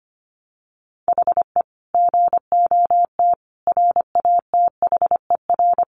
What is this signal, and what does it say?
Morse code, heard as a keyed tone, about 25 wpm, 700 Hz. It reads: HE GOT RATHER
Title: Sentences from Top 700 Words 25wpm